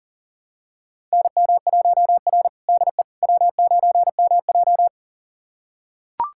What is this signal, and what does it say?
Morse code, heard as a keyed tone, about 40 wpm, 700 Hz. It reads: NM1R DE W9MJ
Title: USA Callsign DE Callsign - 4 Characters - 40wpm